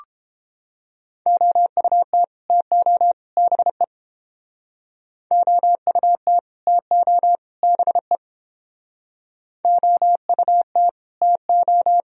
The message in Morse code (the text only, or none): OUT TO BE OUT TO BE OUT TO